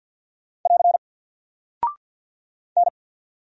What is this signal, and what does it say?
Morse code, heard as a keyed tone, about 48 wpm, 700 Hz. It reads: <AR> N
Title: Single Character (Letter:Number:Punctuation:Prosign) Alt 50wpm